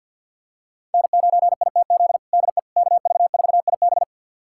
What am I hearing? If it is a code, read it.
Morse code, heard as a keyed tone, about 51 wpm, 700 Hz. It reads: N9ITZ DE KV4IB
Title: USA Callsign DE Callsign - 5 Characters - 50wpm